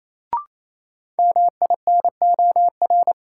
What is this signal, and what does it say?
Morse code, heard as a keyed tone, about 28 wpm, 700 Hz. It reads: MINOR
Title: Sentences from Top 1500 Words - New Word Drill Alt - 28wpm